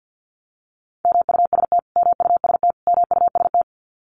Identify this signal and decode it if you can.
Morse code, heard as a keyed tone, about 48 wpm, 700 Hz. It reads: MUST MUST MUST
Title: Top 200 Words Repeated 50wpm